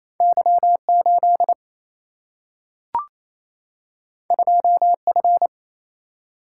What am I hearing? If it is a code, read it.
Morse code, heard as a keyed tone, about 28 wpm, 700 Hz. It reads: Y 8 2 F
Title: Single-Letter Single-Number 28wpm